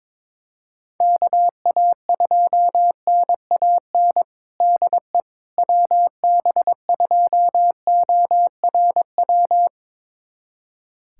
Morse code, heard as a keyed tone, about 22 wpm, 700 Hz. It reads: KA2NAN DE WB2ORW
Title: USA Callsign DE Callsign 22wpm